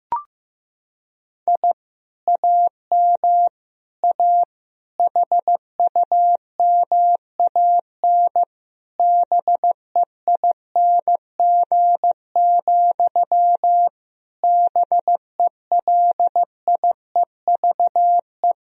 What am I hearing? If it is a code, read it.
Morse code, heard as a keyed tone, about 15 wpm, 700 Hz. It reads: I AM A HUMAN BEING, BELIEVE
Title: Sentences from Top 1000 Words - Review Part 3 - 15wpm